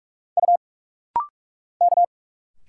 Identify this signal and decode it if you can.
Morse code, heard as a keyed tone, about 44 wpm, 700 Hz. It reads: U K
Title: Single Letters - Rapid-Fire 45wpm